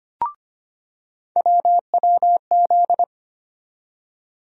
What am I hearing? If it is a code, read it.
Morse code, heard as a keyed tone, about 25 wpm, 700 Hz. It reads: WWZ
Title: USA Call Sign Postfixes - 3 Letters 25wpm